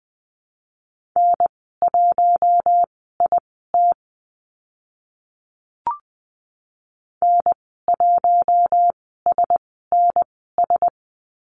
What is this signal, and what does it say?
Morse code, heard as a keyed tone, about 20 wpm, spaced as for 14 wpm, 700 Hz. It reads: N1IT N1SNS
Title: 062 Lesson 11g - Letter h - Call Signs 20wpm